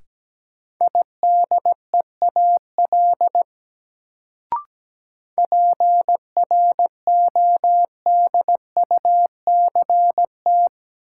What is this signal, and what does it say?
Morse code, heard as a keyed tone, about 17 wpm, 700 Hz. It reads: IDEAL PRODUCT
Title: Sentences from Top 1400 Words - New Word Drill Alt - 17wpm